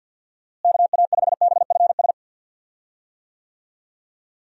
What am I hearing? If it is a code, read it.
Morse code, heard as a keyed tone, about 49 wpm, 700 Hz. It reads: KA5BFS
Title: USA Call Signs - 6 Characters - Repeated 2x - 50wpm